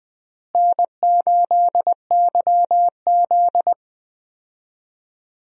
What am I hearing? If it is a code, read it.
Morse code, heard as a keyed tone, about 20 wpm, 700 Hz. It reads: N8YZ
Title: USA Call Signs - 4 Characters - Repeated 3x - 20wpm